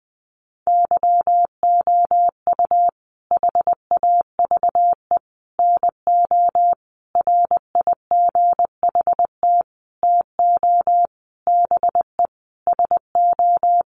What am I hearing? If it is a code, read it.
Morse code, heard as a keyed tone, about 20 wpm, 700 Hz. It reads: YOU HAVE NO RIGHT TO BE SO
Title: Sentences from Top 1100 Words 20wpm